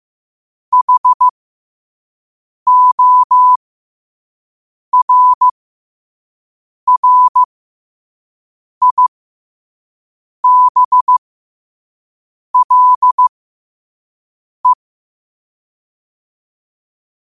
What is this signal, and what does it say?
Morse code, heard as a keyed tone, about 15 wpm, spaced as for 5 wpm, 1000 Hz. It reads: HORRIBLE
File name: qotdfiles / 5wpm